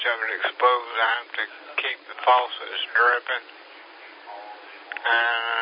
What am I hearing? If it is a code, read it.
Emergency Alert System (SAME) tones (header bursts), heard as harsh buzzing bursts, with speech. It is heard from 0.9 to 5.4 s.